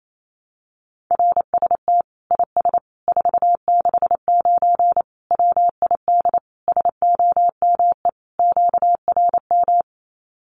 Morse code, heard as a keyed tone, about 28 wpm, 700 Hz. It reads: RST IS 469 WID SOME QRM
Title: Common QSO Phrases 28wpm